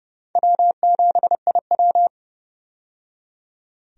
Morse code, heard as a keyed tone, about 30 wpm, 700 Hz. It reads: W7IW
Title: USA Call Signs - 4 Characters - 30wpm